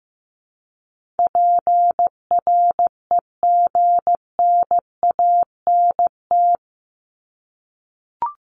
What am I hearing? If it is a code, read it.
Morse code, heard as a keyed tone, about 15 wpm, 700 Hz. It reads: PREGNANT
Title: Top 751-1000 Words 15wpm